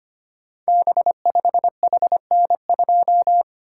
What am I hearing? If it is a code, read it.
Morse code, heard as a keyed tone, about 25 wpm, 700 Hz. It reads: B5HN2
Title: Code Groups - 5 Letter:Numbers - 25wpm